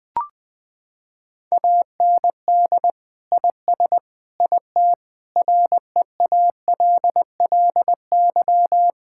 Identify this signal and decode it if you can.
Morse code, heard as a keyed tone, about 20 wpm, 700 Hz. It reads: AND IS IT REALLY
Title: Sentences from Top 1600 Words 20wpm